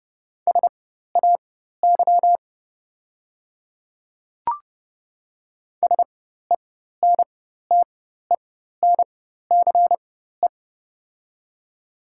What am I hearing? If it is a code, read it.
Morse code, heard as a keyed tone, about 30 wpm, spaced as for 14 wpm, 700 Hz. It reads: SAY SENTENCE
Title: 4x Character Spacing - Top 200 Words - 30wpm